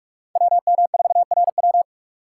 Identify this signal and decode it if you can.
Morse code, heard as a keyed tone, about 45 wpm, 700 Hz. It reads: WM4RW